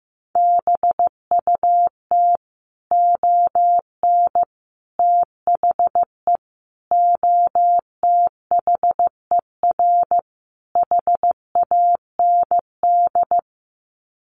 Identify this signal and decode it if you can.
Morse code, heard as a keyed tone, about 15 wpm, 700 Hz. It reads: BUT ON THE OTHER HAND
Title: Sets of 5 Words 15wpm